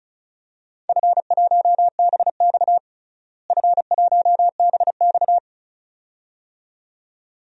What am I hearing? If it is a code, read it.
Morse code, heard as a keyed tone, about 36 wpm, 700 Hz. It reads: F1BX F1BX
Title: DXCC-100 Call Signs - 4 Characters - Repeated 2x - 35wpm